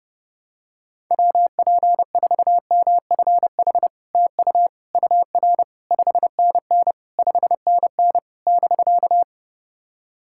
Morse code, heard as a keyed tone, about 32 wpm, 700 Hz. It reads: WP4MFH TU UR 5NN 5NN <BK>
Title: SOTA QSO Practice 30wpm